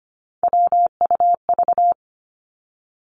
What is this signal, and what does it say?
Morse code, heard as a keyed tone, about 25 wpm, 700 Hz. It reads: WUV
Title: Three Challenging Letter-Number Combinations 25wpm